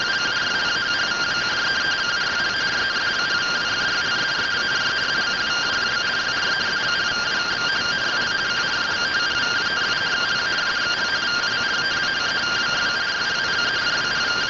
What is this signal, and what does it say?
Two-tone teleprinter signal, about 170 Hz shift.